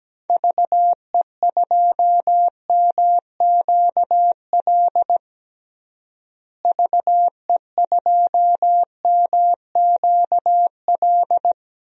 Morse code, heard as a keyed tone, about 17 wpm, 700 Hz. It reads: VE2MQL VE2MQL
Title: DXCC-100 Call Signs - 6 Characters - Repeated 3x - 17wpm